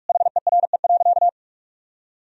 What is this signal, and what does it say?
Morse code, heard as a keyed tone, about 45 wpm, 700 Hz. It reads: HERE.
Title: Sentences from Top 1000 Words - Review Part 3 - 45wpm